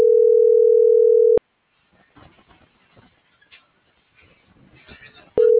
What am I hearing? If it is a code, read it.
Ringback tone (North America), 440 + 480 Hz.